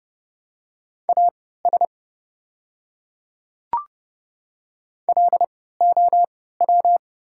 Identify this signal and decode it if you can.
Morse code, heard as a keyed tone, about 30 wpm, spaced as for 17 wpm, 700 Hz. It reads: AS LOW
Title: 3x Character Spacing - Top 200 Words - 30wpm